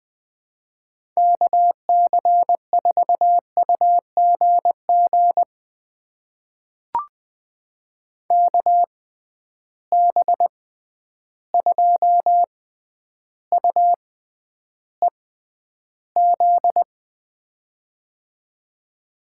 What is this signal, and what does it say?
Morse code, heard as a keyed tone, about 20 wpm, spaced as for 10 wpm, 700 Hz. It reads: KC4UGG KB2UEZ
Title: USA Call Signs - 6 Characters - 6x Character Spacing - 20wpm